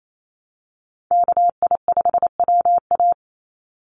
Morse code, heard as a keyed tone, about 28 wpm, 700 Hz. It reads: KI5WA
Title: NA SOTA Chaser Callsigns 28wpm